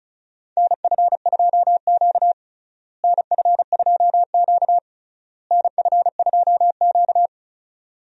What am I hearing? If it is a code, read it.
Morse code, heard as a keyed tone, about 34 wpm, 700 Hz. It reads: NF2Q NF2Q NF2Q
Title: USA Call Signs - 4 Characters - Repeated 3x - 35wpm